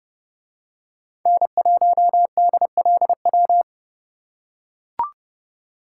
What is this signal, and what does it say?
Morse code, heard as a keyed tone, about 29 wpm, 700 Hz. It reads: N1DLW